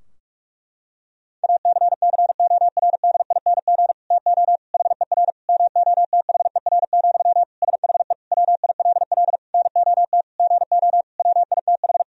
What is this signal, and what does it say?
Morse code, heard as a keyed tone, about 46 wpm, 700 Hz. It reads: ACCORDING TO HER MOTHER, SHE WILL NOT GO WITH
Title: Sentences from Top 1000 Words - Review Part 4 - 45wpm